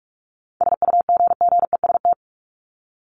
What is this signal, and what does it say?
Morse code, heard as a keyed tone, about 44 wpm, 700 Hz. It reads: SUGGEST
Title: ED Suffix - Top 100 Words with Base Words 45wpm